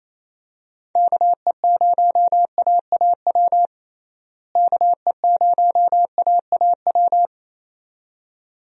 Morse code, heard as a keyed tone, about 27 wpm, 700 Hz. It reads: KE0AAW KE0AAW
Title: USA Call Signs - 6 Characters - Repeated 2x - 28wpm